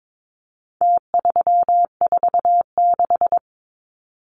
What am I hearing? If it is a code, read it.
Morse code, heard as a keyed tone, about 22 wpm, 700 Hz. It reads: T346
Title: Numbers - Contest Style - 4-Digit 22wpm